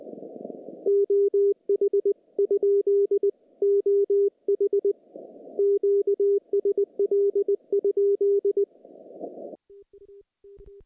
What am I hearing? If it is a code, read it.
Morse code, heard as a keyed tone, about 20 wpm, 400 Hz. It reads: OH ? OH QSL?
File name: contests / CW-SS-k5tr- -0000